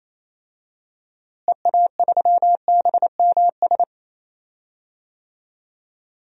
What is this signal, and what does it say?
Morse code, heard as a keyed tone, about 28 wpm, 700 Hz. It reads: EA3BMS
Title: DXCC-100 Call Signs - 6 Characters - Repeated 2x - 28wpm